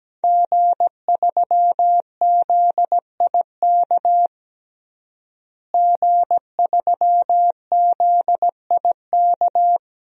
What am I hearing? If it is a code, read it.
Morse code, heard as a keyed tone, about 17 wpm, 700 Hz. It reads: G3ZIK G3ZIK
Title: DXCC-100 Call Signs - 5 Characters - Repeated 2x - 17wpm